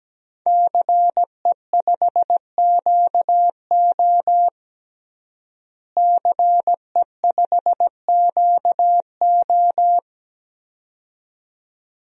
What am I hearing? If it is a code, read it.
Morse code, heard as a keyed tone, about 17 wpm, 700 Hz. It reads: CE5QO CE5QO
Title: DXCC-100 Call Signs - 5 Characters - Repeated 2x - 17wpm